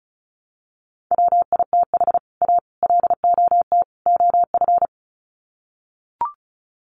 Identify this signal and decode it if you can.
Morse code, heard as a keyed tone, about 34 wpm, 700 Hz. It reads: WITH A LOT OF
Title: Sets of 4 Words 35wpm